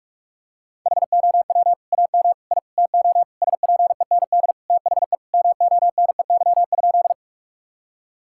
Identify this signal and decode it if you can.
Morse code, heard as a keyed tone, about 46 wpm, 700 Hz. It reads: HOW AM I TO SPEND THE MONEY?
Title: Sentences from Top 1000 Words - Review Part 5 - 45wpm